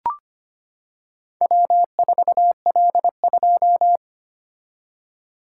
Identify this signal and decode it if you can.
Morse code, heard as a keyed tone, about 25 wpm, 700 Hz. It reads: W4L2
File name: practice / 25wpm / Four Letter-Number Combinations - 25wpm